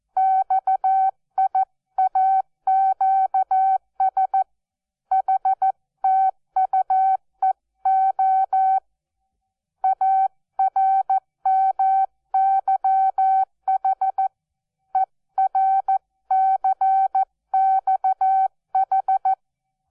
Morse code, heard as a keyed tone, about 14 wpm, 770 Hz. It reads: XIAQS HTUEO ARMYH ERCXH
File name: Onlinelehrgang / morsen / DJ4UF Morsen12